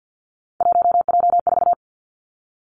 Morse code, heard as a keyed tone, about 50 wpm, 700 Hz. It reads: 1J4
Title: Three Challenging Letter-Number Combinations 50wpm